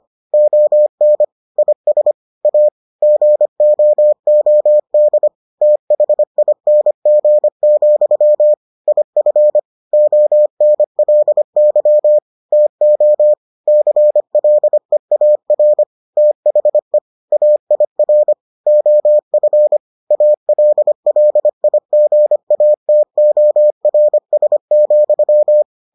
Morse code, heard as a keyed tone, about 25 wpm, 600 Hz. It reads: ON IS A GOOD THING, IF ONLY TO CLEAR THE AIR OF ALLIGATORS,